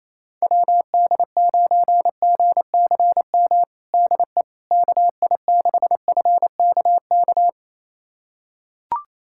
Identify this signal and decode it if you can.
Morse code, heard as a keyed tone, about 28 wpm, 700 Hz. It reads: WD9GCM DE KI6FKK
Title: USA Callsign DE Callsign - 6 Characters - 28wpm